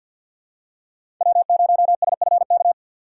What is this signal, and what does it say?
Morse code, heard as a keyed tone, about 50 wpm, 700 Hz. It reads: W0SFK
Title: USA Call Signs - 5 Characters - 6x Character Spacing - 50wpm